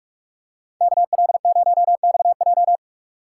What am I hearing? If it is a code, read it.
Morse code, heard as a keyed tone, about 46 wpm, 700 Hz. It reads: KL0XJ